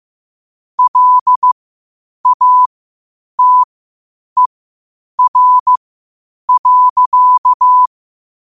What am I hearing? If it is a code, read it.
Morse code, heard as a keyed tone, about 15 wpm, spaced as for 8 wpm, 1000 Hz. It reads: LATER.
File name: qotdfiles / 8wpm